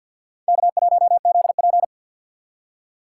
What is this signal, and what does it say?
Morse code, heard as a keyed tone, about 50 wpm, 700 Hz. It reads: K1ZP
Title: USA Call Signs - 4 Characters - 6x Character Spacing - 50wpm